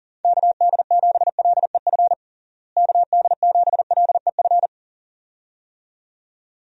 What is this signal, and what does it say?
Morse code, heard as a keyed tone, about 40 wpm, 700 Hz. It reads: KD7LEF KD7LEF
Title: USA Call Signs - 6 Characters - Repeated 2x - 40wpm